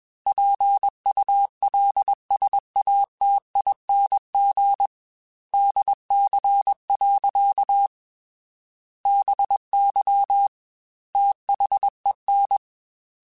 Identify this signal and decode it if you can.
Morse code, heard as a keyed tone, about 21 wpm, 800 Hz. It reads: PULSATING DC. BY THEN
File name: Technical Subjects / Technical Subjects-mp3s / Sw Pwr Sup @20